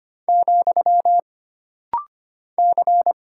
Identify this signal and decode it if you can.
Morse code, heard as a keyed tone, about 25 wpm, 700 Hz. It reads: , C
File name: Single Character (Letter:Number:Punctuation:Prosign) - Rapid-Fire - 25wpm